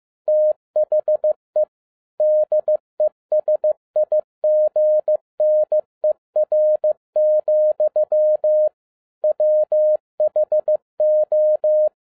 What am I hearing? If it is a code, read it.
Morse code, heard as a keyed tone, about 15 wpm, 610 Hz. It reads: THE DESIGNER, WHO
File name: 2012-01-mp3 / n13-A NEWLY DESIGNED QSL CARD-15wpm